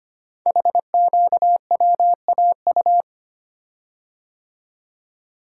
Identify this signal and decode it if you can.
Morse code, heard as a keyed tone, about 25 wpm, 700 Hz. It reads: HQWAU
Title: Code Groups - 5 Letters - 25wpm